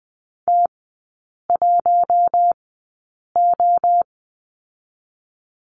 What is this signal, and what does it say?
Morse code, heard as a keyed tone, about 20 wpm, spaced as for 8 wpm, 700 Hz. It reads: T1O
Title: 047 Lesson 9e - Number 4 - Three Characters 20wpm